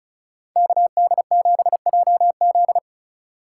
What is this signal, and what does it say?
Morse code, heard as a keyed tone, about 47 wpm, 700 Hz. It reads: KD7JZ